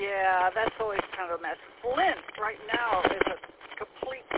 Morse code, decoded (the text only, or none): none